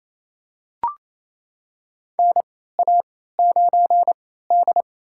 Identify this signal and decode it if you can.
Morse code, heard as a keyed tone, about 28 wpm, spaced as for 16 wpm, 700 Hz. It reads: NA9D
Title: USA Call Signs - 4 Characters - 3x Character Spacing - 28wpm